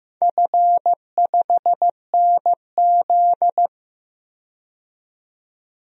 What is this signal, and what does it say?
Morse code, heard as a keyed tone, about 15 wpm, 700 Hz. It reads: F5NZ
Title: DXCC-100 Call Signs - 4 Characters - 15wpm